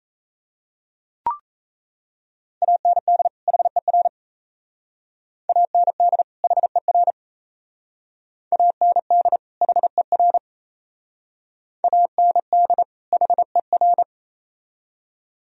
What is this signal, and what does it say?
Morse code, heard as a keyed tone, about 36 wpm, 690 Hz. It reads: AND HER AND HER AND HER AND HER